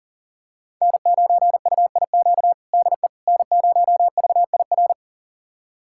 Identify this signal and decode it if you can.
Morse code, heard as a keyed tone, about 40 wpm, 700 Hz. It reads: N9UIQ DE N0VIR
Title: USA Callsign DE Callsign - 5 Characters - 40wpm